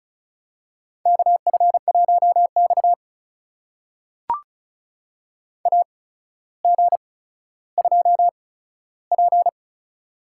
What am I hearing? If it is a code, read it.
Morse code, heard as a keyed tone, about 34 wpm, spaced as for 15 wpm, 700 Hz. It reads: KF1X AG2P
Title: USA Call Signs - 8x Character Spacing - 35wpm